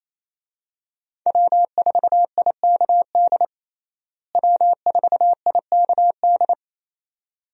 Morse code, heard as a keyed tone, about 28 wpm, 700 Hz. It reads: W4IKD W4IKD
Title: USA Call Signs - 5 Characters - Repeated 2x - 28wpm